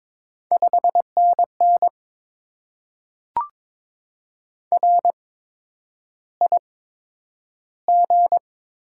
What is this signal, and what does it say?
Morse code, heard as a keyed tone, about 22 wpm, spaced as for 9 wpm, 700 Hz. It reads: 5NN RIG
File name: Top 20 QSO Elements - 8x Character Spacing - 22wpm